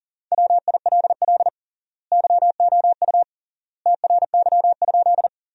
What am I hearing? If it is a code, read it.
Morse code, heard as a keyed tone, about 40 wpm, 700 Hz. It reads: WILL YOU TRY?
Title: 3x Word-Spacing - Sentences from Top 200 Words - 40wpm